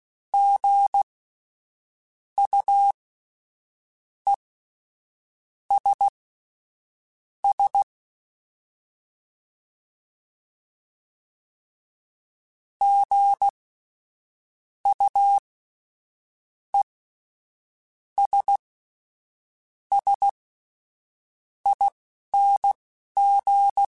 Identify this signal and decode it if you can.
Morse code, heard as a keyed tone, about 16 wpm, spaced as for 6 wpm, 800 Hz. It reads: GUESS GUESSING
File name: Spaced Alphas / Spaced Alphas mp3s / Gs sp @10